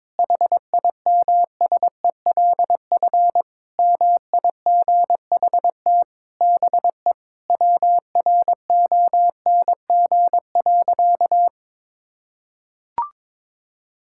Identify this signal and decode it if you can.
Morse code, heard as a keyed tone, about 22 wpm, 700 Hz. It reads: HIMSELF MIGHT BE WRONG.
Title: Sentences from Top 1600 Words 22wpm